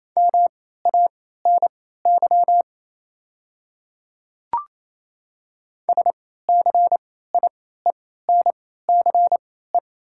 Morse code, heard as a keyed tone, about 28 wpm, spaced as for 16 wpm, 700 Hz. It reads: MANY SCIENCE
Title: 3x Character Spacing - Top 500 Words - 28wpm